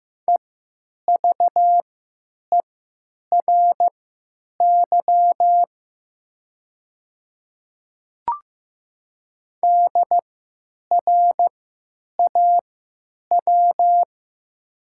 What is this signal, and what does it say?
Morse code, heard as a keyed tone, about 15 wpm, spaced as for 9 wpm, 700 Hz. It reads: EVERY DRAW
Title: 3x Character Spacing - Top 300 Words - 15wpm